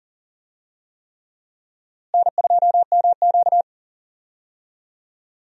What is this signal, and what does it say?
Morse code, heard as a keyed tone, about 38 wpm, 700 Hz. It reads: N2MQ
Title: USA Call Signs - 4 Characters - Repeated 3x - 40wpm